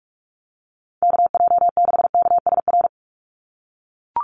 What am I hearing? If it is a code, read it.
Morse code, heard as a keyed tone, about 44 wpm, 700 Hz. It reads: KJ6KSR
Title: USA Call Signs 45wpm